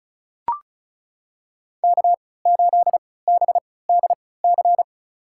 Morse code, heard as a keyed tone, about 36 wpm, spaced as for 20 wpm, 700 Hz. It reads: K8BDC
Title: USA Call Signs - 5 Characters - 3x Character Spacing - 35wpm